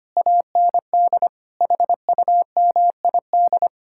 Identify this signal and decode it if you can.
Morse code, heard as a keyed tone, about 33 wpm, 700 Hz. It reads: AND HUMID